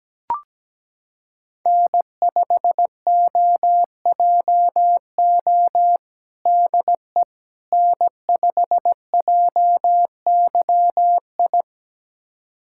Morse code, heard as a keyed tone, about 17 wpm, 700 Hz. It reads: N5OJO DE N5JYI
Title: USA Callsign DE Callsign - 5 Characters - 17wpm